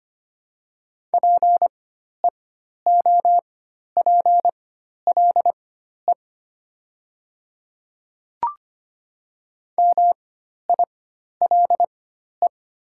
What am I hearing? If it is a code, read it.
Morse code, heard as a keyed tone, about 25 wpm, spaced as for 12 wpm, 700 Hz. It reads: PEOPLE MILE
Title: 4x Character Spacing - Top 300 Words - 25wpm